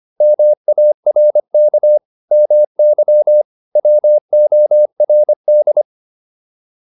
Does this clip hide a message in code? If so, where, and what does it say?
Yes: Morse code from 0.2 to 5.8 s, heard as a keyed tone, about 25 wpm, 600 Hz. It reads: MARK MY WORD